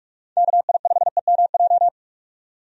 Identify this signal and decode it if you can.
Morse code, heard as a keyed tone, about 46 wpm, 700 Hz. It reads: KI5EMJ
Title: USA Call Signs - 6 Characters - 45wpm